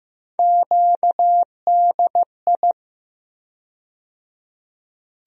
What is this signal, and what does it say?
Morse code, heard as a keyed tone, about 15 wpm, 700 Hz. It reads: QDI
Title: USA Call Sign Postfixes - 3 Letters 15wpm